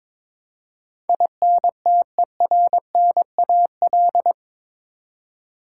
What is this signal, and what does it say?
Morse code, heard as a keyed tone, about 22 wpm, 700 Hz. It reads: INTERNAL